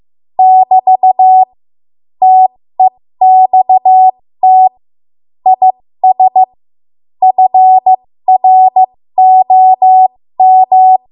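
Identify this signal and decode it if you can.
Morse code, heard as a keyed tone, about 15 wpm, spaced as for 13 wpm, 750 Hz. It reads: <BT> TEXT IS FROM
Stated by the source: ARRL Morse code practice at 13 WPM.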